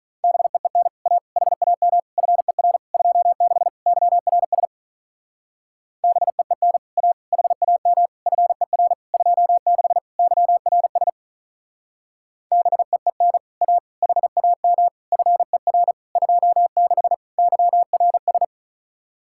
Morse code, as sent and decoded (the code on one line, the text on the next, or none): -... . -. / .- / .... .- -- / ..-. .-. / ..--- -.... / -.-- .-. ... / -... . -. / .- / .... .- -- / ..-. .-. / ..--- -.... / -.-- .-. ... / -... . -. / .- / .... .- -- / ..-. .-. / ..--- -.... / -.-- .-. ...
BEEN A HAM FER 26 YRS BEEN A HAM FER 26 YRS BEEN A HAM FER 26 YRS